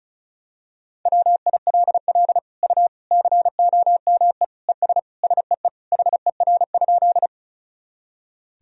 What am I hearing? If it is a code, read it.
Morse code, heard as a keyed tone, about 36 wpm, 700 Hz. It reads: WILL U COME ES SEE HER?
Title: Sentences from Top 100 Words and Abbreviations 35wpm